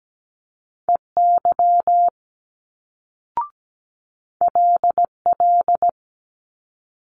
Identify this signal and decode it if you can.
Morse code, heard as a keyed tone, about 17 wpm, 700 Hz. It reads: EY LL
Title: Multi-letter Phonemes Quick Fire 17wpm